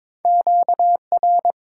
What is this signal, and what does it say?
Morse code, heard as a keyed tone, about 22 wpm, 700 Hz. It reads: Q R
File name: Top 90 QSO Elements - 8x Character Spacing - 22wpm